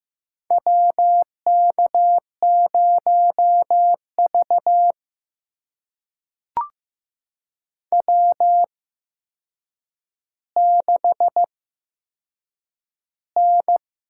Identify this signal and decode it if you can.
Morse code, heard as a keyed tone, about 15 wpm, spaced as for 7 wpm, 700 Hz. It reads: WK0V W6N
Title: USA Call Signs - 4 Characters - 8x Character Spacing - 15wpm